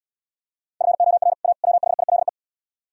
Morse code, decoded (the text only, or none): FOR A WHILE